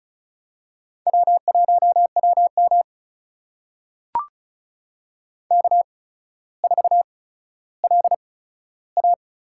Morse code, heard as a keyed tone, about 36 wpm, spaced as for 15 wpm, 700 Hz. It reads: W1WM K4LA
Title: USA Call Signs - 8x Character Spacing - 35wpm